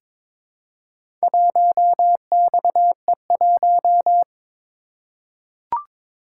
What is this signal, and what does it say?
Morse code, heard as a keyed tone, about 22 wpm, 700 Hz. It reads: 1XE1
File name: practice / 22wpm / Four Letter-Number Combinations - 22wpm